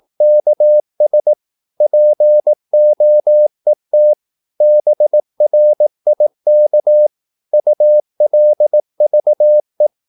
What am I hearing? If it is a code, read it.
Morse code, heard as a keyed tone, about 18 wpm, 600 Hz. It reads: KS POET BRIK ULVE
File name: Korte ord DK 18wpm